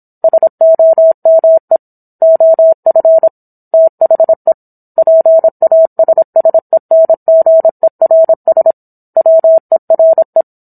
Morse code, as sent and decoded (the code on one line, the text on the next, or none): ... --- -- . / --- ..-. / - .... / .--. .- ... ... -. --. .-. ... / .-- . .-. .
SOME OF THE PASSENGERS WERE